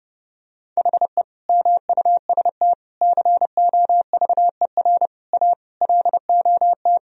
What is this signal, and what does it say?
Morse code, heard as a keyed tone, about 30 wpm, 700 Hz. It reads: HE MUST COVER A LOT
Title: Sentences from Top 1000 Words - Review Part 3 - 30wpm